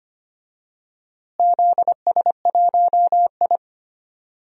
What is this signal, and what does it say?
Morse code, heard as a keyed tone, about 25 wpm, 700 Hz. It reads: ZS1I
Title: DXCC-100 Call Signs - 4 Characters - Repeated 2x - 25wpm